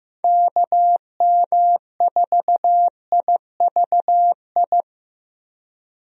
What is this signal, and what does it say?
Morse code, heard as a keyed tone, about 16 wpm, 700 Hz. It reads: KM4IVI
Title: USA Call Signs - 6 Characters - Repeated 2x - 15wpm